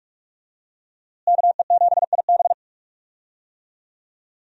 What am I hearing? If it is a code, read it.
Morse code, heard as a keyed tone, about 45 wpm, 700 Hz. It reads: KE7IB
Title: USA Call Signs - 5 Characters - Repeated 3x - 45wpm